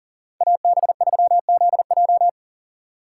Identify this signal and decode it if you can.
Morse code, heard as a keyed tone, about 40 wpm, 700 Hz. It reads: AB3ZJ
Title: USA Call Signs - 5 Characters - 3x Character Spacing - 40wpm